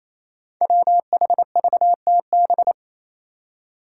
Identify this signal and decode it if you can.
Morse code, heard as a keyed tone, about 28 wpm, 700 Hz. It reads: WHVTB
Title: Five Challenging Letter-Number Combinations 28wpm